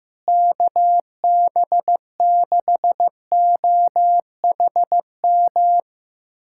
Morse code, heard as a keyed tone, about 15 wpm, 700 Hz. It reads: KB6OHM
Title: USA Callsign DE Callsign 15wpm